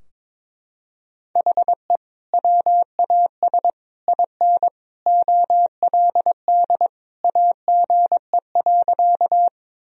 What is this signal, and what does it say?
Morse code, heard as a keyed tone, about 22 wpm, 700 Hz. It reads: HE WAS IN OLD AGE.
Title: Sentences from Top 1000 Words - Review Part 3 - 22wpm